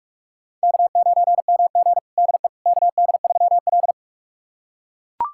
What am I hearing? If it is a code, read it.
Morse code, heard as a keyed tone, about 41 wpm, 700 Hz. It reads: K9MG DE KD3L
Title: USA Callsign DE Callsign - 4 Characters - 45wpm